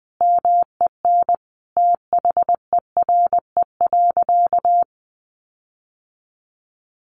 Morse code, heard as a keyed tone, about 20 wpm, 700 Hz. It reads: MEN THERE.
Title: Sentences from Top 1500 Words 20wpm